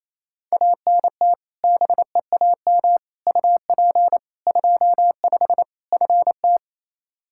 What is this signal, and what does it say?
Morse code, heard as a keyed tone, about 27 wpm, 700 Hz. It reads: ANT BEAM UP 25 FT
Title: K1USN SST QSO Practice 28wpm